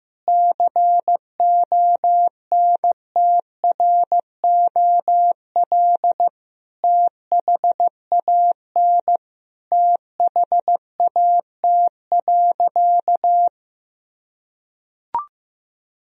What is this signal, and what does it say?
Morse code, heard as a keyed tone, about 15 wpm, 700 Hz. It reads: CONTROL THAN THAT.
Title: Sentences from Top 1000 Words - Review Part 3 - 15wpm